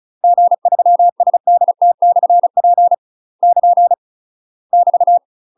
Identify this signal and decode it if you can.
Morse code, heard as a keyed tone, about 36 wpm, 700 Hz. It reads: G3SDT/P <KN> <BT>
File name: UK MASTER QSOs TRAINING / 35WPM / UK MASTER QSOs 0010